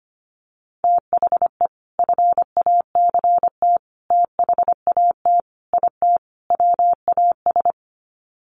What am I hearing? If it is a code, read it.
Morse code, heard as a keyed tone, about 25 wpm, 700 Hz. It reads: THE FACT THAT IT WAS
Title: Sets of 5 Words Encore2 25wpm